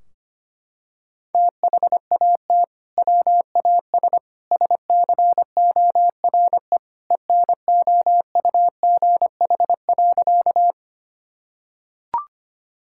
Morse code, heard as a keyed tone, about 25 wpm, 700 Hz. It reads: THAT WAS SCORE ENOUGH.
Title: Sentences from Top 1600 Words 25wpm